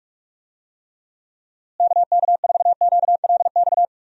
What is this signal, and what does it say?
Morse code, heard as a keyed tone, about 41 wpm, 700 Hz. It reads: KK4QLX
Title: USA Call Signs - 6 Characters - Repeated 3x - 45wpm